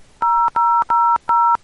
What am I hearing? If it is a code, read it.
DTMF, heard as telephone touch tones, tones of about 280 ms. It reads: 0000